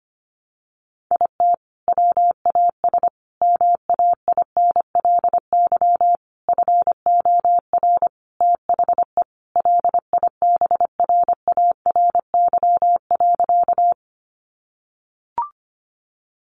Morse code, heard as a keyed tone, about 25 wpm, 700 Hz. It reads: IT WAS MAINLY FOR THE LIBRARY.
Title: Sentences from Top 1400 Words 25wpm